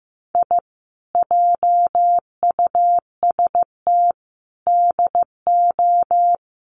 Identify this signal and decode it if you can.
Morse code, heard as a keyed tone, about 15 wpm, 700 Hz. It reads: I JUST DO
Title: Sets of 4 Words 15wpm